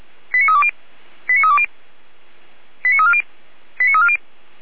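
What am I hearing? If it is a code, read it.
Five-tone selective call (ZVEI), heard as quick runs of five tones, tones of about 70 ms. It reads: 89320 89320 89340 89340
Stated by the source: Five-tone ZVEI selective call received on SDR.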